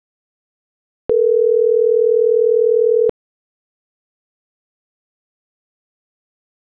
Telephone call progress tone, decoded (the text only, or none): ringback tone